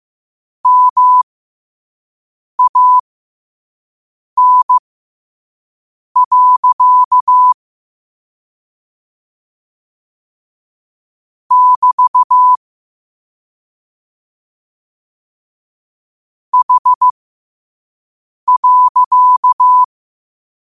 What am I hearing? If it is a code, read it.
Morse code, heard as a keyed tone, about 15 wpm, spaced as for 5 wpm, 1000 Hz. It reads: MAN. <BT> H.
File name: qotdfiles / 5wpm